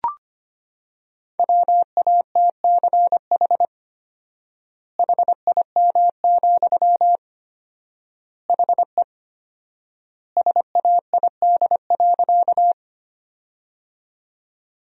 Morse code, heard as a keyed tone, about 25 wpm, 700 Hz. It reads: WATCH HIM, HE SAID.
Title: 4x Word-Spacing - Sentences from Top 300 Words - 25wpm